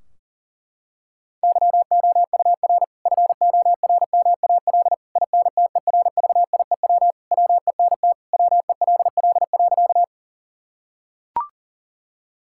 Morse code, heard as a keyed tone, about 40 wpm, 700 Hz. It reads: YOUR FORMAL INTERVIEW WENT WELL.
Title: Sentences from Top 1600 Words 40wpm